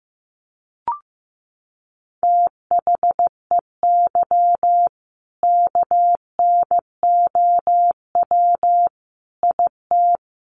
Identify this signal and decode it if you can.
Morse code, heard as a keyed tone, about 15 wpm, 700 Hz. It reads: THEY KNOW IT
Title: Sentences from Top 1200 Words 15wpm